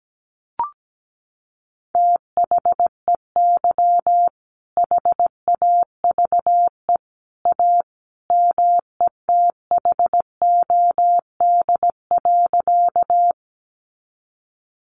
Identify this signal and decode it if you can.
Morse code, heard as a keyed tone, about 17 wpm, 700 Hz. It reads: THEY HAVE A METHOD.
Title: Sentences from Top 600 Words 17wpm